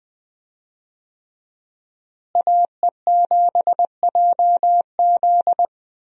Morse code, heard as a keyed tone, about 21 wpm, 700 Hz. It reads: AE7JZ
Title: USA Call Signs - 5 Characters - Repeated 3x - 20wpm